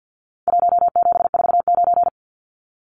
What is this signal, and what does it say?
Morse code, heard as a keyed tone, about 50 wpm, 700 Hz. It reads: 1749
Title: Numbers - Contest Style - 4-Digit 50wpm